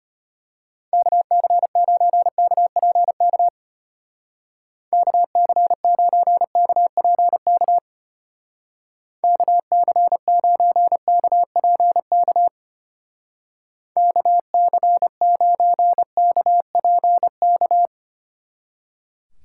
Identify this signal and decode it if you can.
Morse code, heard as a keyed tone, about 32 wpm, 700 Hz. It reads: KC9KPK KC9KPK KC9KPK KC9KPK